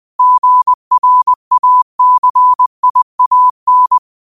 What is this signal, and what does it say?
Morse code, heard as a keyed tone, about 20 wpm, 1000 Hz. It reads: GRACIAN
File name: qotdfiles / 20wpm